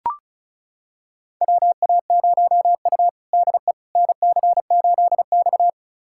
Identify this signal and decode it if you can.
Morse code, heard as a keyed tone, about 34 wpm, 700 Hz. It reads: WA0U DE NC8X
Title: USA Callsign DE Callsign - 4 Characters - 35wpm